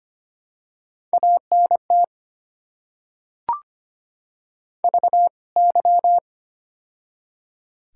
Morse code, heard as a keyed tone, about 25 wpm, 700 Hz. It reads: ANT VY
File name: Top 90 QSO Elements - 2x Character Spacing - 25wpm